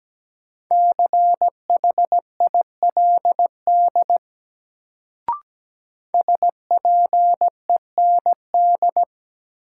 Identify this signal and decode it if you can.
Morse code, heard as a keyed tone, about 17 wpm, 700 Hz. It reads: CHILD SPEND
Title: Sentences from Top 800 Words - New Word Drill - 17wpm